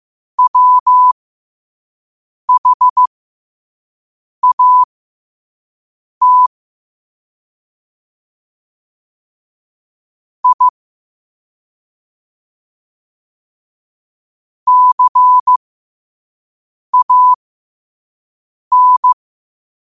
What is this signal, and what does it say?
Morse code, heard as a keyed tone, about 15 wpm, spaced as for 5 wpm, 1000 Hz. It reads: WHAT I CAN
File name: qotdfiles / 5wpm